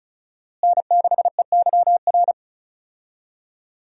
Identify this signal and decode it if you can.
Morse code, heard as a keyed tone, about 36 wpm, 700 Hz. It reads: N6EYR
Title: USA Call Signs - 5 Characters - Repeated 2x - 35wpm